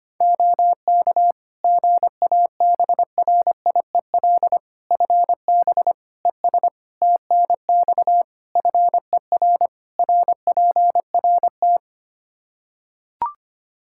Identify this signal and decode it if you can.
Morse code, heard as a keyed tone, about 25 wpm, 700 Hz. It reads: OK GABRIEL FB ES TNX FER RPRT
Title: Practice QSO - Exchange 2 - Info - 25wpm